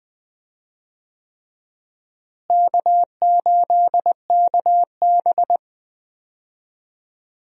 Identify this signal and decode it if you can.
Morse code, heard as a keyed tone, about 21 wpm, 700 Hz. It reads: K8KB
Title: USA Call Signs - 4 Characters - Repeated 3x - 20wpm